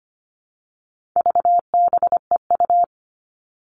Morse code, heard as a keyed tone, about 25 wpm, 700 Hz. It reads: VBEU
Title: All Cut Numbers - 4-Digit 25wpm